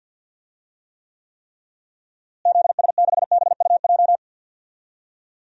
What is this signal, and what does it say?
Morse code, heard as a keyed tone, about 50 wpm, 700 Hz. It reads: ZS6BUJ
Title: DXCC-100 Call Signs - 6 Characters - Repeated 3x - 50wpm